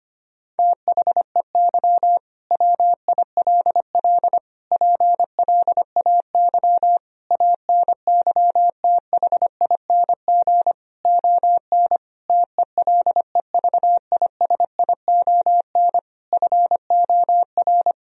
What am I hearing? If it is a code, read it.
Morse code, heard as a keyed tone, about 25 wpm, 700 Hz. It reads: THEY WILL PLAY ANYTHING ON TELEVISION FOR
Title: Sentences from Top 1000 Words - Review Part 4 - 25wpm